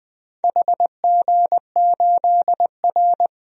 Morse code, heard as a keyed tone, about 20 wpm, 700 Hz. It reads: HG8R
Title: DXCC-100 Call Signs - 4 Characters - 20wpm